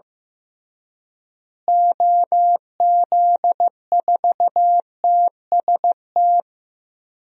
Morse code, heard as a keyed tone, about 15 wpm, 700 Hz. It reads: OZ4TST
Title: DXCC-100 Call Signs - 6 Characters - Repeated 3x - 15wpm